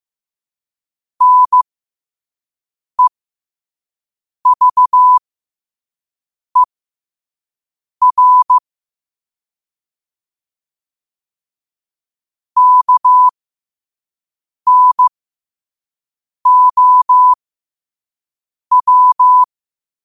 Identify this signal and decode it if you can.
Morse code, heard as a keyed tone, about 15 wpm, spaced as for 5 wpm, 1000 Hz. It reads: NEVER KNOW